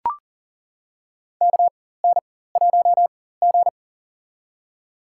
Morse code, heard as a keyed tone, about 40 wpm, spaced as for 19 wpm, 700 Hz. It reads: KN1G
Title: USA Call Signs - 4x Character Spacing - 40wpm